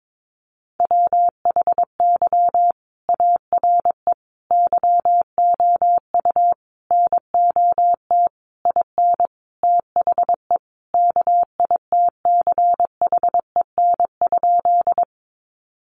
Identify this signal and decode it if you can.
Morse code, heard as a keyed tone, about 22 wpm, 700 Hz. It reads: WHY ARE YOU NOT IN THE KITCHEN?
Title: Sentences from Top 1400 Words 22wpm